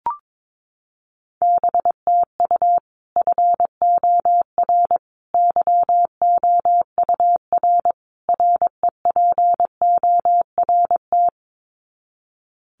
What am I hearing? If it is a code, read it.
Morse code, heard as a keyed tone, about 22 wpm, 700 Hz. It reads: BTU FOR YOUR REPORT
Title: Common QSO Phrases 22wpm